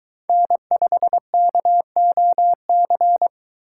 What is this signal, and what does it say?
Morse code, heard as a keyed tone, about 22 wpm, 700 Hz. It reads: N5KOC